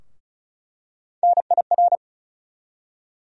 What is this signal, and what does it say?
Morse code, heard as a keyed tone, about 36 wpm, 700 Hz. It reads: NIR
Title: USA Call Sign Postfixes - 3 Letters 35wpm